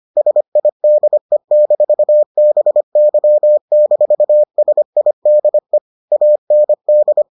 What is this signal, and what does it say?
Morse code, heard as a keyed tone, about 25 wpm, 600 Hz. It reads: SIDE-BY-SIDE AND